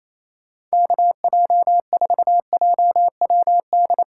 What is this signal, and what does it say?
Morse code, heard as a keyed tone, about 36 wpm, 700 Hz. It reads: KJ4JWD